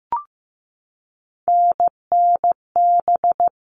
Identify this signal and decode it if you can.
Morse code, heard as a keyed tone, about 15 wpm, 700 Hz. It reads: NNB
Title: All Cut Numbers - 3-Digit 15wpm